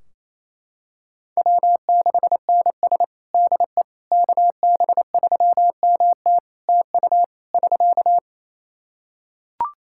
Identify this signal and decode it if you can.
Morse code, heard as a keyed tone, about 28 wpm, 700 Hz. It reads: W6NS DE KB3MT TU <SK>
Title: Practice QSO - Exchange 4 - Ending - 28wpm